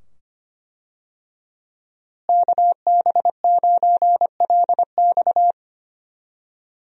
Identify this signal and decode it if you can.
Morse code, heard as a keyed tone, about 25 wpm, 700 Hz. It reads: KB9LX
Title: USA Call Signs - 5 Characters - Repeated 2x - 25wpm